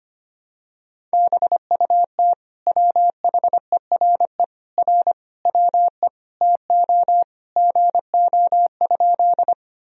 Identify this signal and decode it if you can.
Morse code, heard as a keyed tone, about 25 wpm, 700 Hz. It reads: BUT WHERE R WE TO GO?
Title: Sentences from Top 200 Words and Abbreviations 25wpm